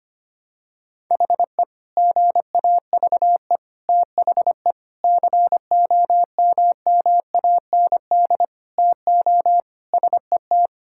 Morse code, heard as a keyed tone, about 25 wpm, 700 Hz. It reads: HE GAVE THE COMMAND TO SET
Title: Sentences from Top 1600 Words 25wpm